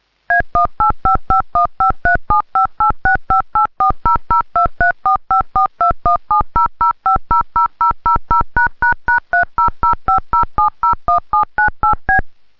DTMF, heard as telephone touch tones, tones of about 130 ms. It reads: A18551537586584002345421700500000###30050701798B